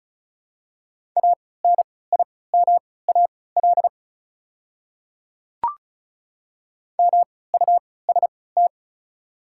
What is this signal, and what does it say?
Morse code, heard as a keyed tone, about 36 wpm, spaced as for 20 wpm, 700 Hz. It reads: ANIMAL MUST
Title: 3x Character Spacing - Top 200 Words - 35wpm